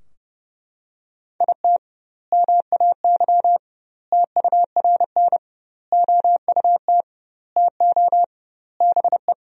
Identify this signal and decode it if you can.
Morse code, heard as a keyed tone, about 30 wpm, 700 Hz. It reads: IT MAY TURN OUT TO BE
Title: 2x Word-Spacing - Sentences from Top 400 Words - 30wpm